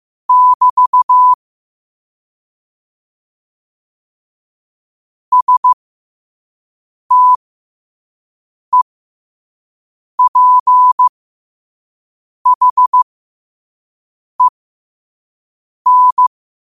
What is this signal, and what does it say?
Morse code, heard as a keyed tone, about 15 wpm, spaced as for 5 wpm, 1000 Hz. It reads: <BT> STEPHEN
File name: qotdfiles / 5wpm